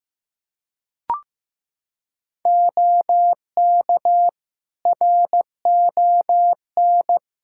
Morse code, heard as a keyed tone, about 15 wpm, 700 Hz. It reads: OK RON
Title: Practice QSO - Exchange 4 - Ending - 15wpm